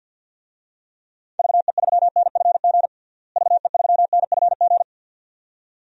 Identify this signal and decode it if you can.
Morse code, heard as a keyed tone, about 49 wpm, 700 Hz. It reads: VE3NFG VE3NFG
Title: DXCC-100 Call Signs - 6 Characters - Repeated 2x - 50wpm